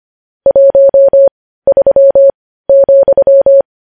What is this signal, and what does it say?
Morse code, heard as a keyed tone, about 25 wpm, spaced as for 15 wpm, 550 Hz. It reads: 13,
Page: News Headlines in Morse Code at 15 WPM